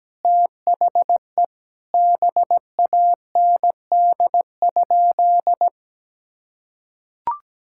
Morse code, heard as a keyed tone, about 17 wpm, 700 Hz. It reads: THE BAND?
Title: Sentences from Top 1600 Words 17wpm